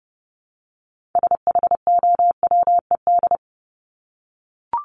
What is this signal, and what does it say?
Morse code, heard as a keyed tone, about 30 wpm, 700 Hz. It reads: SHOWED
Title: ED Suffix - Top 100 Words with Base Words 30wpm